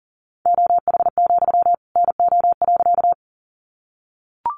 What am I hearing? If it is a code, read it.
Morse code, heard as a keyed tone, about 40 wpm, 700 Hz. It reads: OH, NO.
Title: Sentences from Top 500 Words 40wpm